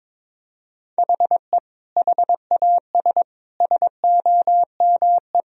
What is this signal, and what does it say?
Morse code, heard as a keyed tone, about 22 wpm, 700 Hz. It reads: HE HAS SOME
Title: Sentences from Top 1000 Words - Review Part 3 - 22wpm